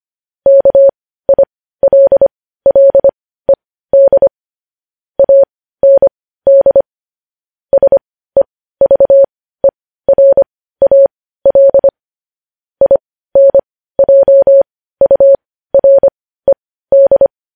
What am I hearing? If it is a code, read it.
Morse code, heard as a keyed tone, about 25 wpm, spaced as for 15 wpm, 550 Hz. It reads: KILLED AND SEVERAL INJURED